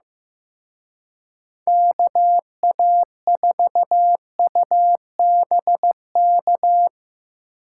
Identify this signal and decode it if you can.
Morse code, heard as a keyed tone, about 17 wpm, 700 Hz. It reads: KA4UBK
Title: Speed Racing - USA 2x3 Call Signs 15wpm